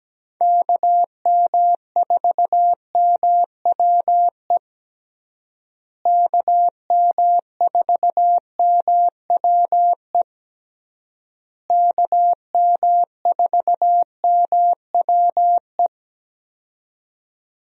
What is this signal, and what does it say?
Morse code, heard as a keyed tone, about 17 wpm, 700 Hz. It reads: KM4MWE KM4MWE KM4MWE
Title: USA Call Signs - 6 Characters - Repeated 3x - 17wpm